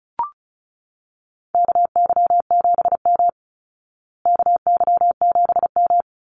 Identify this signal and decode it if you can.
Morse code, heard as a keyed tone, about 34 wpm, 700 Hz. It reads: KY7M KY7M
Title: Call Signs Repeated 3x - 35wpm